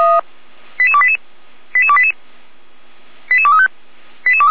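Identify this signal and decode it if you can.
Five-tone selective call (ZVEI), heard as quick runs of five tones, tones of about 70 ms. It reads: 88288 88288 88235
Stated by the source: ZVEI five-tone call received by SDR, from audio.f85360.feuerwehr-orlamuende.org, a Feuerwehr Orlamünde fire brigade server.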